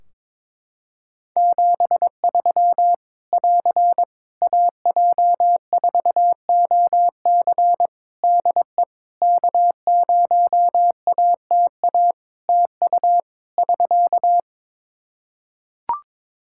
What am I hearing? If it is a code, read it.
Morse code, heard as a keyed tone, about 22 wpm, 700 Hz. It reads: 73 <AR> AJ4OC DE K0ATA TU <SK>
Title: Practice QSO - Exchange 4 - Ending - 22wpm